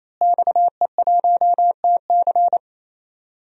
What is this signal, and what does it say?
Morse code, heard as a keyed tone, about 28 wpm, 700 Hz. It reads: XE1TC
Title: DXCC-100 Call Signs - 5 Characters - Repeated 3x - 28wpm